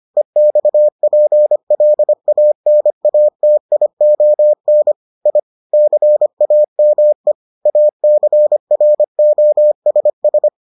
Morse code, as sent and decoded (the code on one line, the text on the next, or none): . -..- .--. .-.. .- -. .- - .. --- -. / .. / -.-. .- -- . / .- -.-. .-. --- ... ...
EXPLANATION I CAME ACROSS